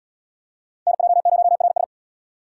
Morse code, heard as a keyed tone, about 66 wpm, 700 Hz. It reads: NJ0RS